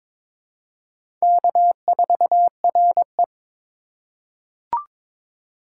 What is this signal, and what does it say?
Morse code, heard as a keyed tone, about 22 wpm, 700 Hz. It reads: K4RE
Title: USA Call Signs - 4 Characters - 4x Character Spacing - 22wpm